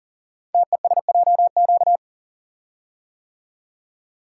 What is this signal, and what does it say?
Morse code, heard as a keyed tone, about 39 wpm, 700 Hz. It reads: TESJQ
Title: Code Groups - 5 Letters - 40wpm